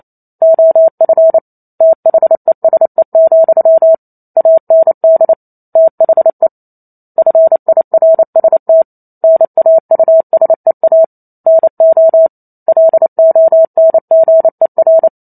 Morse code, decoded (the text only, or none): OF THESE, AND THE FIRST NAUSEA NO LONGER